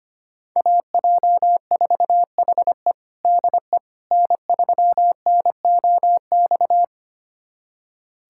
Morse code, heard as a keyed tone, about 25 wpm, 700 Hz. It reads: AJ4HE DE N3NOX
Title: USA Callsign DE Callsign - 5 Characters - 25wpm